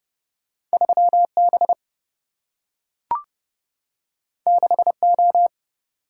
Morse code, heard as a keyed tone, about 30 wpm, 700 Hz. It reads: 3B 6O
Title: Single-Letter Single-Number 30wpm